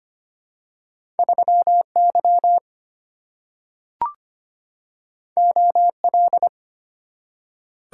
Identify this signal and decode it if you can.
Morse code, heard as a keyed tone, about 25 wpm, 700 Hz. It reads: 3Y OL
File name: Two Letter-Number Combinations - 25wpm